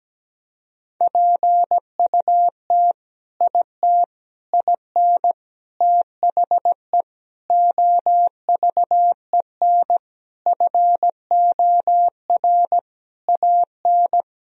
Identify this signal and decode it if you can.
Morse code, heard as a keyed tone, about 17 wpm, 700 Hz. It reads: PUT IT IN THE OVEN FOR AN
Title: Sentences from Top 1000 Words - Review Part 4 - 17wpm